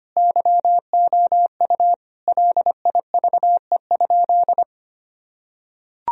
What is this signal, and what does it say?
Morse code, heard as a keyed tone, about 25 wpm, 700 Hz. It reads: YOU LIVE?
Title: Sentences from Top 1000 Words - Review Part 4 - 25wpm